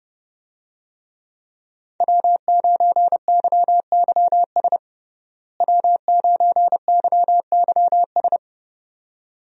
Morse code, heard as a keyed tone, about 32 wpm, 700 Hz. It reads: W9YYS W9YYS
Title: USA Call Signs - 5 Characters - Repeated 2x - 30wpm